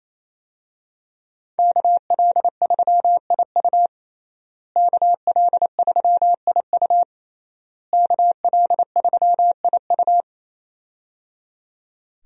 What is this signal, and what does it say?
Morse code, heard as a keyed tone, about 28 wpm, 700 Hz. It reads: KL3IU KL3IU KL3IU
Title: USA Call Signs - 5 Characters - Repeated 3x - 28wpm